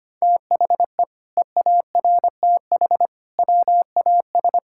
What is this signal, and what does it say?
Morse code, heard as a keyed tone, about 25 wpm, 700 Hz. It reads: THE EARTH WAS
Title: Sentences from Top 1000 Words - Review Part 6 - 25wpm